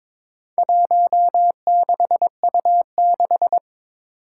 Morse code, heard as a keyed tone, about 22 wpm, 700 Hz. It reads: 16U6
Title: Four Challenging Letter-Number Combinations 22wpm